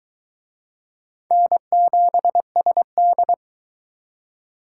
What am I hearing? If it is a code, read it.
Morse code, heard as a keyed tone, about 23 wpm, 700 Hz. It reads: N7SD